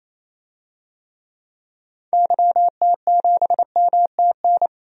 Morse code, heard as a keyed tone, about 28 wpm, 700 Hz. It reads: YT7MTN
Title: DXCC-100 Call Signs - 6 Characters - 28wpm